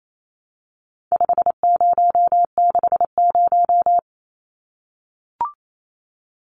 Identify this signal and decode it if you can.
Morse code, heard as a keyed tone, about 28 wpm, 700 Hz. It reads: 5060
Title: Numbers - 4-Digit 28wpm